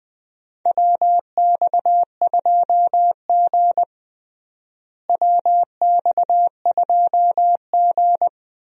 Morse code, heard as a keyed tone, about 20 wpm, 700 Hz. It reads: WX2G WX2G
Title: USA Call Signs - 4 Characters - Repeated 2x - 20wpm